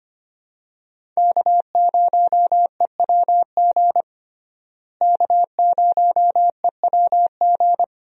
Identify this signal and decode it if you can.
Morse code, heard as a keyed tone, about 25 wpm, 700 Hz. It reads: K0EWG K0EWG
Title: USA Call Signs - 5 Characters - Repeated 3x - 25wpm